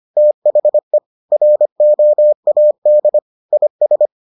Morse code, heard as a keyed tone, about 25 wpm, 600 Hz. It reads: THE ROAD IS